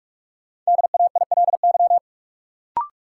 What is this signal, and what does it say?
Morse code, heard as a keyed tone, about 46 wpm, 700 Hz. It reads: DAILY
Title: Sentences from Top 1100 Words - New Word Drill Alt - 45wpm